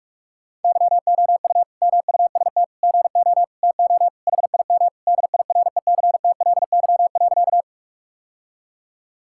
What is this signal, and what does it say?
Morse code, heard as a keyed tone, about 46 wpm, 700 Hz. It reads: YOU MUST GO TO HIM DIRECTLY.
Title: Sentences from Top 1000 Words - Review Part 5 - 45wpm